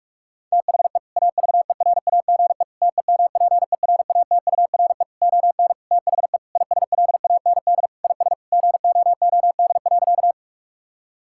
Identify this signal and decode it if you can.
Morse code, heard as a keyed tone, about 46 wpm, 700 Hz. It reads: THE AVERAGE TEMPERATURE ON THE ISLAND IS GOOD.
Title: Sentences from Top 1000 Words - Review Part 6 - 45wpm